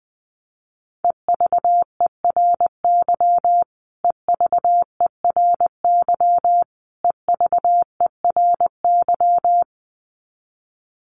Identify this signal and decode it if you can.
Morse code, heard as a keyed tone, about 20 wpm, 700 Hz. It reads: EVERY EVERY EVERY
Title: Top 500 Words Repeated 20wpm